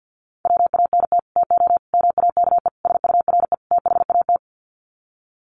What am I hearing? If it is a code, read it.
Morse code, heard as a keyed tone, about 50 wpm, 700 Hz. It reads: WANT TO MAKE SURE THAT
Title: Sets of 5 Words 50wpm